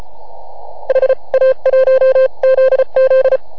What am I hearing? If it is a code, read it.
Morse code, heard as a keyed tone, about 30 wpm, 520 Hz. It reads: HA1ZZ